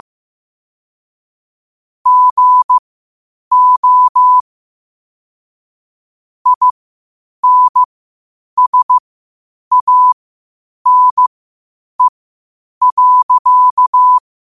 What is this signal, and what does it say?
Morse code, heard as a keyed tone, about 15 wpm, spaced as for 8 wpm, 1000 Hz. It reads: GO INSANE.
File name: qotdfiles / 8wpm